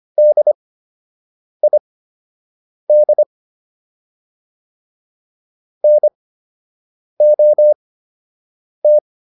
Morse code, heard as a keyed tone, about 25 wpm, spaced as for 7 wpm, 600 Hz. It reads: DID NOT